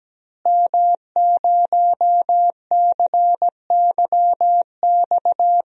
Morse code, heard as a keyed tone, about 17 wpm, 700 Hz. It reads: M0CYX
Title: DXCC-100 Call Signs - 5 Characters - Repeated 3x - 17wpm